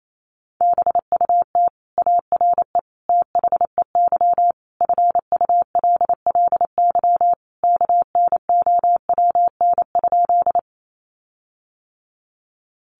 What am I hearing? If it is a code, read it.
Morse code, heard as a keyed tone, about 28 wpm, 700 Hz. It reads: BUT ARE THEY FULLY KNOWN?
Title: Sentences from Top 1300 Words 28wpm